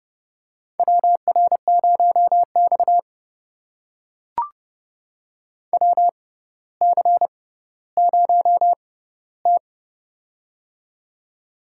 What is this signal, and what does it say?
Morse code, heard as a keyed tone, about 30 wpm, spaced as for 15 wpm, 700 Hz. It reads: WR0X WC0T
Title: USA Call Signs - 4 Characters - 6x Character Spacing - 30wpm